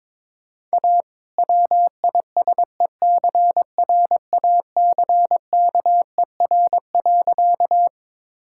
Morse code, heard as a keyed tone, about 22 wpm, 690 Hz. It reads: A WISECRACKER.
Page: morse-code-ninja-practice-22wpm